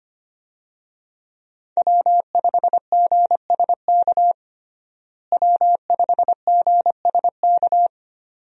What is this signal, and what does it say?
Morse code, heard as a keyed tone, about 25 wpm, 700 Hz. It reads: W5GSK W5GSK
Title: USA Call Signs - 5 Characters - Repeated 2x - 25wpm